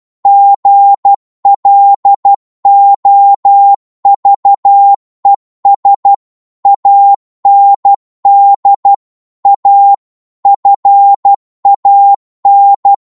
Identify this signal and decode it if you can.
Morse code, heard as a keyed tone, about 12 wpm, 800 Hz. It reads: GLOVES AND A FAN